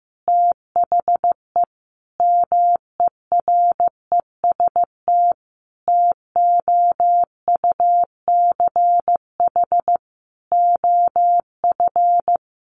Morse code, heard as a keyed tone, about 16 wpm, 700 Hz. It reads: THE MEREST TOUCH OF
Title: A Sound Of Thunder 15wpm